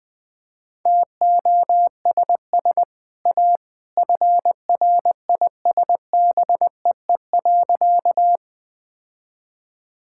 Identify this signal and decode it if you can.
Morse code, heard as a keyed tone, about 21 wpm, 700 Hz. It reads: TOSS A FRISBEE.
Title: Easy Sentences with Hints - 20wpm